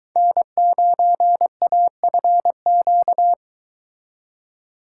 Morse code, heard as a keyed tone, about 23 wpm, 700 Hz. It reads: N9AFQ